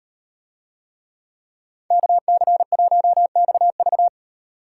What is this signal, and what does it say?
Morse code, heard as a keyed tone, about 38 wpm, 700 Hz. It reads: KC1XV